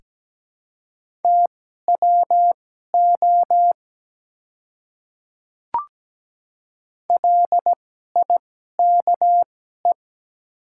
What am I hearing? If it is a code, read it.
Morse code, heard as a keyed tone, about 17 wpm, spaced as for 12 wpm, 700 Hz. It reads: TWO LIKE
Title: 2x Character Spacing - Top 100 Words - 17wpm